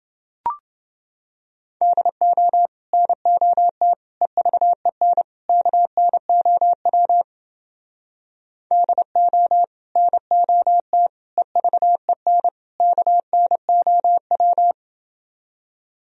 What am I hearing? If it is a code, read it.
Morse code, heard as a keyed tone, about 30 wpm, 700 Hz. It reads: DO NOT EVEN KNOW DO NOT EVEN KNOW